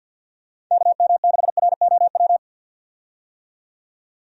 Morse code, heard as a keyed tone, about 47 wpm, 700 Hz. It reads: KM6ROW
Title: USA Call Signs - 6 Characters - 50wpm